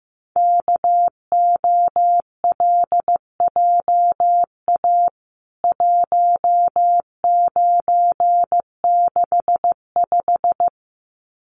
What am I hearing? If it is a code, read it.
Morse code, heard as a keyed tone, about 15 wpm, 700 Hz. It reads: KOLJA 1965
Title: CWT Exchange 15wpm